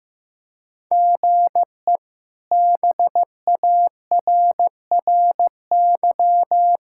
Morse code, heard as a keyed tone, about 16 wpm, 700 Hz. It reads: GE BARRY
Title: K1USN SST QSO Practice - 15wpm